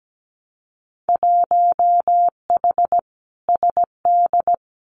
Morse code, heard as a keyed tone, about 17 wpm, 700 Hz. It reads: 1H SD
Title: Winter Field Day Exchanges 17wpm